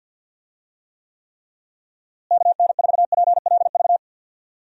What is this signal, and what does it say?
Morse code, heard as a keyed tone, about 45 wpm, 700 Hz. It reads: KN4PLV
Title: USA Call Signs - 6 Characters - Repeated 3x - 50wpm